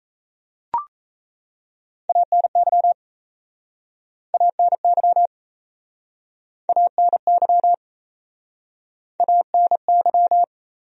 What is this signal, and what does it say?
Morse code, heard as a keyed tone, about 39 wpm, 700 Hz. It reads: ANY ANY ANY ANY